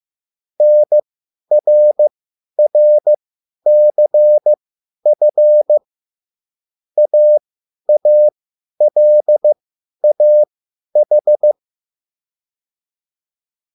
Morse code, heard as a keyed tone, about 15 wpm, spaced as for 10 wpm, 600 Hz. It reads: NRRCF AALAH